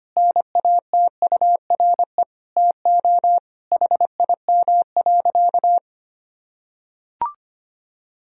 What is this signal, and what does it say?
Morse code, heard as a keyed tone, about 25 wpm, 700 Hz. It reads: NATURE TO HIM.
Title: Sentences from Top 1000 Words - Review Part 1 - 25wpm